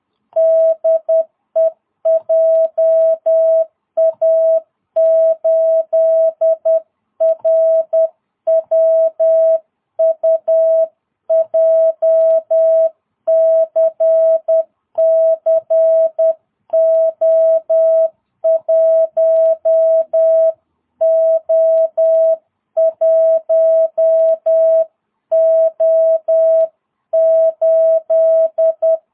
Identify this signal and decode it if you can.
Morse code, heard as a keyed tone, about 10 wpm, 650 Hz. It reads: DE JA8RWU JCC O1O1O8
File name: jh8cbh / 2022cw